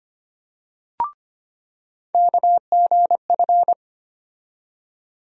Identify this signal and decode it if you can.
Morse code, heard as a keyed tone, about 25 wpm, 700 Hz. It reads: KGF
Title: USA Call Sign Postfixes - 3 Letters 25wpm